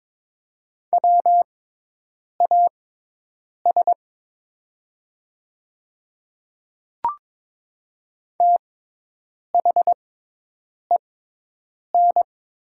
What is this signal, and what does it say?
Morse code, heard as a keyed tone, about 22 wpm, spaced as for 8 wpm, 700 Hz. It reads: WAS THEN
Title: 6x Character Spacing - Top 100 Words - 22wpm